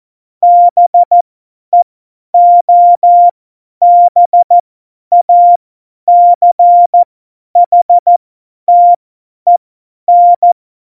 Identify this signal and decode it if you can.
Morse code, heard as a keyed tone, about 14 wpm, spaced as for 10 wpm, 700 Hz. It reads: BEOBACHTEN
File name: Morsekurs / Witze / Witz 04. September 2018, 50 F70